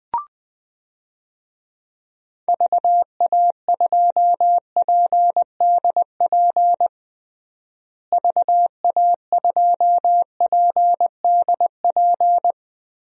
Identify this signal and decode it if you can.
Morse code, heard as a keyed tone, about 20 wpm, 700 Hz. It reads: VA2PDP VA2PDP
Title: DXCC-100 Call Signs - 6 Characters - Repeated 2x - 20wpm